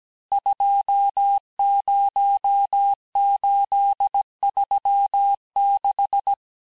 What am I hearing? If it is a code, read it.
Morse code, heard as a keyed tone, about 17 wpm, 800 Hz. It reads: 20836
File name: corrige-cours-1581-texte-chiffre-vitesse-16-mpm